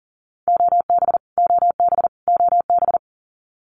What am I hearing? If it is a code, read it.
Morse code, heard as a keyed tone, about 40 wpm, 700 Hz. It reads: OB OB OB
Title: Abbreviations Repeated 3x - 40wpm